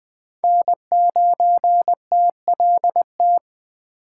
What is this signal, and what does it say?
Morse code, heard as a keyed tone, about 20 wpm, 700 Hz. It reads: N9TLT